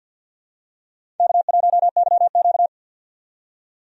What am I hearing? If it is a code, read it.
Morse code, heard as a keyed tone, about 50 wpm, 700 Hz. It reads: K1YQ
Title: USA Call Signs - 4 Characters - 50wpm